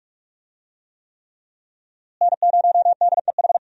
Morse code, heard as a keyed tone, about 42 wpm, 700 Hz. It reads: N0DEH
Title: USA Call Signs - 5 Characters - Repeated 3x - 45wpm